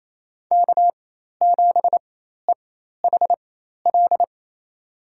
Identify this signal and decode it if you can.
Morse code, heard as a keyed tone, about 28 wpm, spaced as for 13 wpm, 700 Hz. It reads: K7EHL
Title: USA Call Signs - 5 Characters - 4x Character Spacing - 28wpm